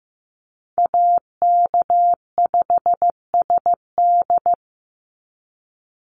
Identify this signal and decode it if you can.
Morse code, heard as a keyed tone, about 15 wpm, 700 Hz. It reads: AK5SD
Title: NA SOTA Chaser Callsigns 15wpm